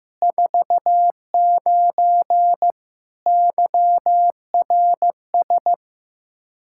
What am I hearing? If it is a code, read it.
Morse code, heard as a keyed tone, about 15 wpm, 700 Hz. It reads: 49 YRS
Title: Practice QSO - Exchange 3 - More Info - 15wpm